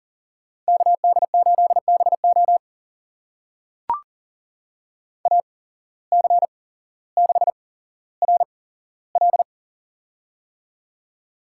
Morse code, heard as a keyed tone, about 43 wpm, spaced as for 17 wpm, 700 Hz. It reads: KD8BO AC6RL
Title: USA Call Signs - 5 Characters - 8x Character Spacing - 40wpm